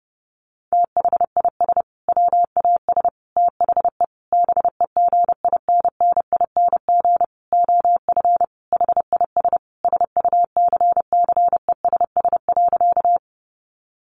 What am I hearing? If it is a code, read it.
Morse code, heard as a keyed tone, about 30 wpm, 700 Hz. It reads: THIS WAS THE BEGINNING OF HIS SUCCESS.
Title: Sentences from Top 900 Words 30wpm